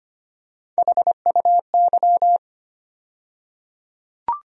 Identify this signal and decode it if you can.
Morse code, heard as a keyed tone, about 25 wpm, 700 Hz. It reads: HUY
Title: USA Call Sign Postfixes - 3 Letters 25wpm